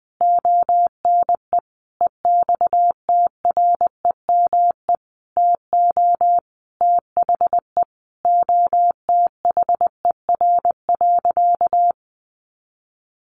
Morse code, heard as a keyed tone, about 19 wpm, 700 Hz. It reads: ONE EXTREME TO THE OTHER.
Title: Sentences from Top 1300 Words 20wpm